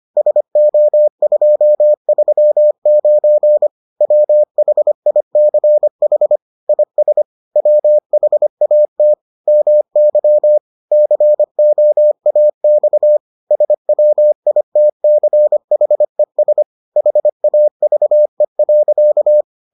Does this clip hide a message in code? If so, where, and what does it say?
Yes: Morse code from 0.2 to 19.4 s, heard as a keyed tone, about 25 wpm, 600 Hz. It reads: SO239 WHICH IS WHAT MY COAX SWITCHES HAVE.